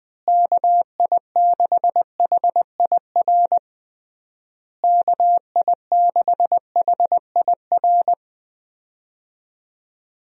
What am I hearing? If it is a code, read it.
Morse code, heard as a keyed tone, about 20 wpm, 700 Hz. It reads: KI6HIR KI6HIR
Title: USA Call Signs - 6 Characters - Repeated 3x - 20wpm